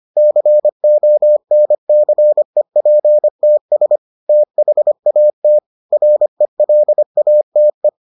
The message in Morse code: CONCEPTS THAT RELATE

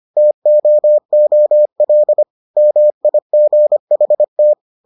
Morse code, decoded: TOOL MIGHT

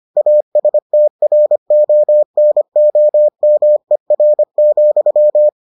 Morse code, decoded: ASTRONOMER,

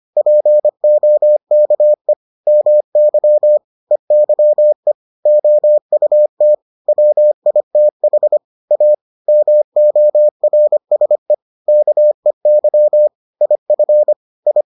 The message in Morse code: POKE MY EYE OUT WITH A MORSE KEY IF I

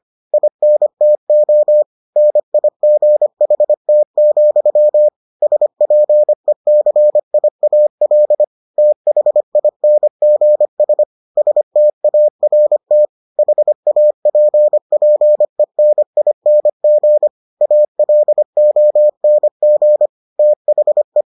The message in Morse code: INTO NIGHT, SPECIAL THINGS START HAPPENING ALONG THE

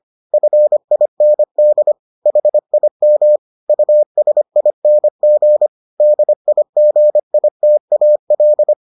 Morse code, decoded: FIND HIM USING DIGITAL

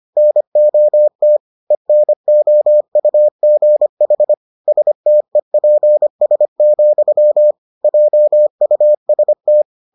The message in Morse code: NOT ENOUGH STEPS, JUST

